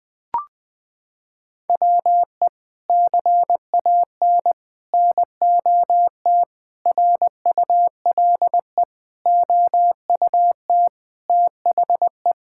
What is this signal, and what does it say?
Morse code, heard as a keyed tone, about 20 wpm, 700 Hz. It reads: WE CAN NOT RULE OUT THE
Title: Sets of 8 Words - 20wpm